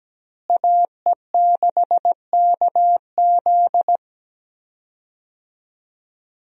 Morse code, heard as a keyed tone, about 17 wpm, 700 Hz. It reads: AE6KZ
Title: USA Call Signs - 5 Characters - Repeated 2x - 17wpm